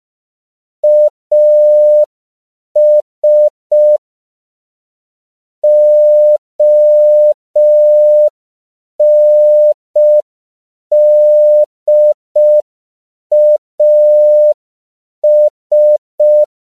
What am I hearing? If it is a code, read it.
Morse code, heard as a keyed tone, about 5 wpm, 600 Hz. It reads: AS ONDAS
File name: CW / 5PPM